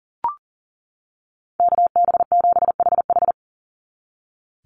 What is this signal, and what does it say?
Morse code, heard as a keyed tone, about 40 wpm, 700 Hz. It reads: KB7HH
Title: NA SOTA Chaser Callsigns 40wpm